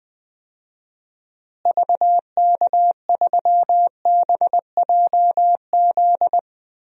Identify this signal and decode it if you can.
Morse code, heard as a keyed tone, about 20 wpm, 700 Hz. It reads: VK3BJZ
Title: DXCC-100 Call Signs - 6 Characters - Repeated 3x - 20wpm